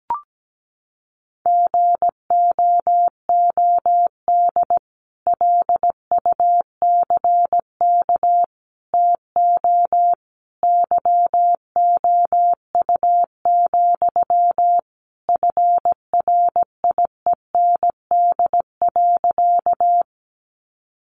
Morse code, heard as a keyed tone, about 17 wpm, 700 Hz. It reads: GOOD LUCK TO YOU, FRIEND.
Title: Sentences from Top 1500 Words 17wpm